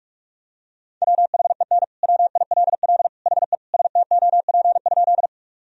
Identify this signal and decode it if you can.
Morse code, heard as a keyed tone, about 42 wpm, 700 Hz. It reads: WHEN WILL HE STOP?
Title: Sentences from Top 1000 Words - Review Part 1 - 45wpm